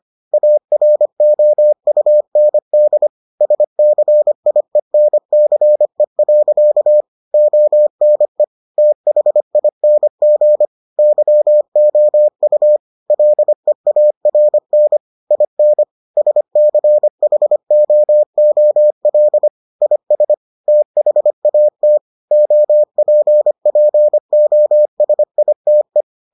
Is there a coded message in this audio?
Yes: Morse code from 0.3 to 26.0 s, heard as a keyed tone, about 25 wpm, 600 Hz. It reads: AROUND SCIENCE. ONE THING YOU LEARN IN SCHOOL IS THAT OPPOSITE